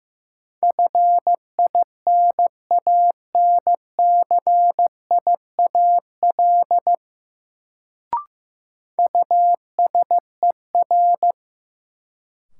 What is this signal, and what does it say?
Morse code, heard as a keyed tone, about 15 wpm, 700 Hz. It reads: FINANCIAL USER
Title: Sentences from Top 1000 Words - Review Part 5 - New Word Drill - 15wpm